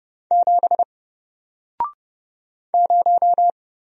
Morse code, heard as a keyed tone, about 31 wpm, 700 Hz. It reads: 7 0
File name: Single-Letter-Number-30wpm